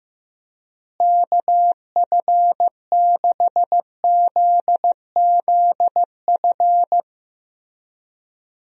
Morse code, heard as a keyed tone, about 15 wpm, 700 Hz. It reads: KF6ZZF
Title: USA Call Signs - 6 Characters - Repeated 3x - 15wpm